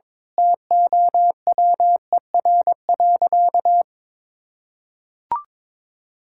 Morse code, heard as a keyed tone, about 21 wpm, 700 Hz. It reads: TOWER.
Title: Pangram Challenge v2 - 22wpm